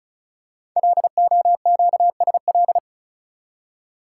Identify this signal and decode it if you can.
Morse code, heard as a keyed tone, about 36 wpm, 700 Hz. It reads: LOQSL
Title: Code Groups - 5 Letters - 35wpm